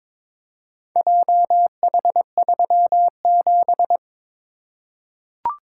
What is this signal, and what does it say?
Morse code, heard as a keyed tone, about 22 wpm, 700 Hz. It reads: JH37
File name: practice / 22wpm / Four Letter-Number Combinations - 22wpm